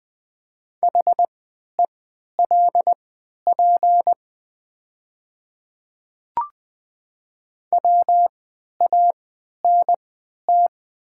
Morse code, heard as a keyed tone, about 20 wpm, spaced as for 11 wpm, 690 Hz. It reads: HELP WANT
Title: 3x Character Spacing - Top 300 Words - 20wpm